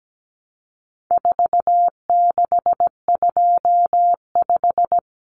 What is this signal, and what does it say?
Morse code, heard as a keyed tone, about 17 wpm, 700 Hz. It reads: 4625
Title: Common Cut Numbers - 4-Digit 17wpm